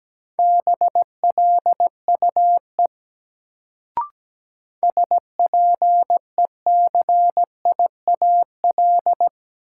Morse code, heard as a keyed tone, about 17 wpm, 700 Hz. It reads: BLUE SPECIAL
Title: Sentences from Top 1000 Words - Review Part 3 - New Word Drill - 17wpm